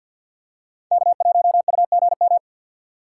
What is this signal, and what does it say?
Morse code, heard as a keyed tone, about 48 wpm, 700 Hz. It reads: K1UGM
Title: USA Call Signs - 5 Characters - 2x Character Spacing - 50wpm